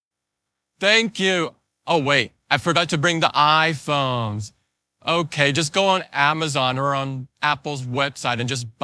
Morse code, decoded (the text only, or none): none